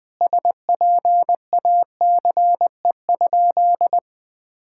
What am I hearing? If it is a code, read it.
Morse code, heard as a keyed tone, about 20 wpm, 700 Hz. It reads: SPACE?
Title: Sentences from Top 1000 Words - Review Part 5 - 20wpm